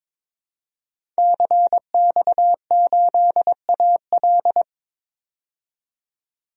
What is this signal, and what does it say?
Morse code, heard as a keyed tone, about 22 wpm, 700 Hz. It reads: CX8AL
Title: DXCC-100 Call Signs - 5 Characters - 22wpm